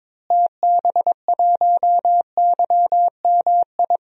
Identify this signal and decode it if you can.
Morse code, heard as a keyed tone, about 22 wpm, 700 Hz. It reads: TB1YMI
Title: DXCC-100 Call Signs - 6 Characters - Repeated 2x - 22wpm